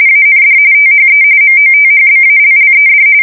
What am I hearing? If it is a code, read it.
RTTY, about 45.45 baud. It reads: AND THE MOTHBALLED
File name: fsk170 / battleshipnj